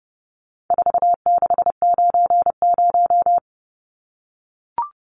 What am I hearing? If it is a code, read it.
Morse code, heard as a keyed tone, about 29 wpm, 700 Hz. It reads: 4690
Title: Numbers - 4-Digit 30wpm